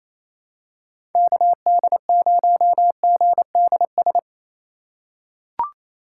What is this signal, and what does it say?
Morse code, heard as a keyed tone, about 28 wpm, 700 Hz. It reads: KD0GDS
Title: USA Call Signs - 6 Characters - 2x Character Spacing - 28wpm